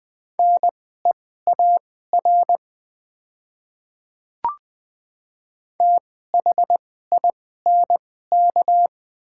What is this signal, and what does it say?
Morse code, heard as a keyed tone, about 20 wpm, spaced as for 15 wpm, 700 Hz. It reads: NEAR THINK
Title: 2x Character Spacing - Top 500 Words - 20wpm